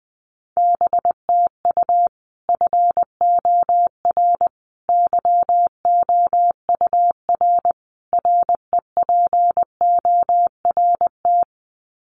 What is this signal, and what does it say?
Morse code, heard as a keyed tone, about 20 wpm, 700 Hz. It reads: BTU FOR YOUR REPORT
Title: Common QSO Phrases 20wpm